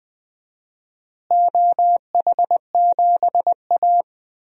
Morse code, heard as a keyed tone, about 20 wpm, 700 Hz. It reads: OH7A
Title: DXCC-100 Call Signs - 4 Characters - Repeated 2x - 20wpm